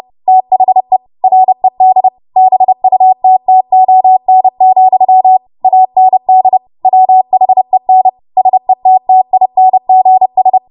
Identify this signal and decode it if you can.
Morse code, heard as a keyed tone, about 30 wpm, 750 Hz. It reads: THE RED BUTTON, AND WHEN SETTINGS
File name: Morse / 30WPM